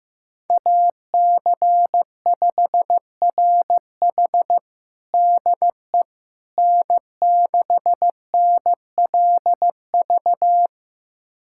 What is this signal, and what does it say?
Morse code, heard as a keyed tone, about 16 wpm, 700 Hz. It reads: AC5RH DE N6NLV
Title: USA Callsign DE Callsign 15wpm